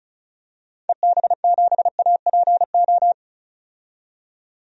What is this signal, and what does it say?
Morse code, heard as a keyed tone, about 36 wpm, 700 Hz. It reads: EB7APO
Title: DXCC-100 Call Signs - 6 Characters - 35wpm